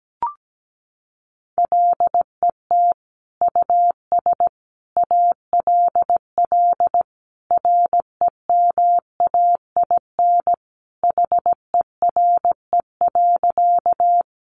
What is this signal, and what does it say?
Morse code, heard as a keyed tone, about 17 wpm, 700 Hz. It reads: LET US ALL REMAIN HERE.
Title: Sentences from Top 1200 Words 17wpm